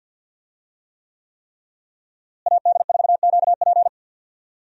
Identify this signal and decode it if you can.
Morse code, heard as a keyed tone, about 52 wpm, 700 Hz. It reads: AD4QP
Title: USA Call Signs - 5 Characters - Repeated 3x - 50wpm